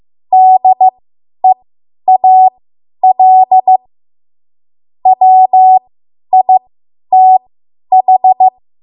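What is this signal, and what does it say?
Morse code, heard as a keyed tone, about 15 wpm, spaced as for 10 wpm, 750 Hz. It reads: DEAL WITH
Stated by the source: Morse code practice run at 10 WPM from a code practice archive.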